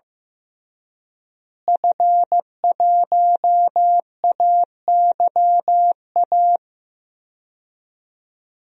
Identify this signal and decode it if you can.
Morse code, heard as a keyed tone, about 15 wpm, 700 Hz. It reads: F1AYA
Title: DXCC-100 Call Signs - 5 Characters - Repeated 2x - 15wpm